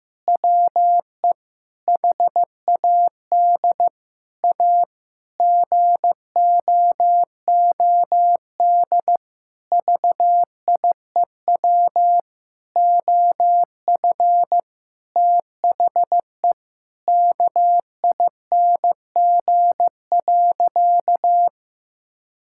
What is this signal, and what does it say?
Morse code, heard as a keyed tone, about 16 wpm, 700 Hz. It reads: WE HAD A GOOD VIEW OF THE KING.
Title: Sentences from Top 1000 Words - Review Part 6 - 15wpm